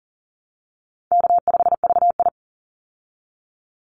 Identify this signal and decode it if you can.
Morse code, heard as a keyed tone, about 40 wpm, 700 Hz. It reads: K5VI
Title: Call Signs Repeated 2x - 40wpm